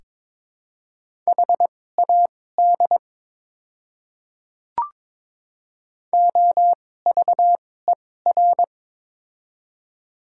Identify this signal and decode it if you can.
Morse code, heard as a keyed tone, about 22 wpm, spaced as for 16 wpm, 700 Hz. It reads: HAD OVER
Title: 2x Character Spacing - Top 200 Words - 22wpm